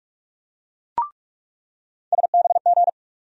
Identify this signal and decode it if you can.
Morse code, heard as a keyed tone, about 44 wpm, 700 Hz. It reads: SBG
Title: USA Call Sign Postfixes - 3 Letters 45wpm